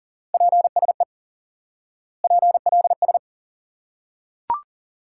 Morse code, heard as a keyed tone, about 38 wpm, 700 Hz. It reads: PSE PLS
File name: IQE Course 3 - Lesson 89 - YAGI - Two Elements - 40wpm